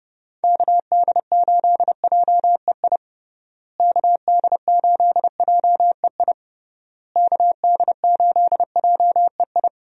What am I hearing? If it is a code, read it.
Morse code, heard as a keyed tone, about 31 wpm, 700 Hz. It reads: KD8JEI KD8JEI KD8JEI
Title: USA Call Signs - 6 Characters - Repeated 3x - 30wpm